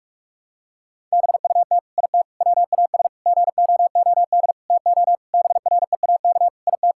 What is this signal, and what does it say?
Morse code, heard as a keyed tone, about 46 wpm, 700 Hz. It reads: BUT IT WAS GOOD TO BREAK IT
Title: Sentences from Top 1000 Words - Review Part 6 - 45wpm